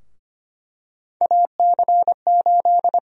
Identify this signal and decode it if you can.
Morse code, heard as a keyed tone, about 25 wpm, 700 Hz. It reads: AC8
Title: USA Call Sign Prefixes - Repeated 2x - 25wpm